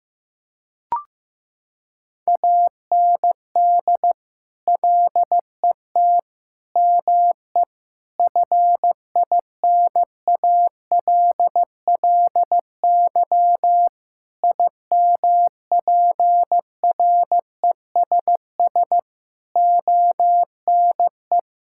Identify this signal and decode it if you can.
Morse code, heard as a keyed tone, about 15 wpm, 700 Hz. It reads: AND LET ME FINALLY IMPRESS ONE
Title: Sentences from Top 1600 Words 15wpm